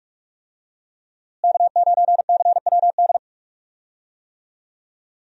Morse code, heard as a keyed tone, about 46 wpm, 700 Hz. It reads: K9CWD
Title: USA Call Signs - 5 Characters - Repeated 2x - 45wpm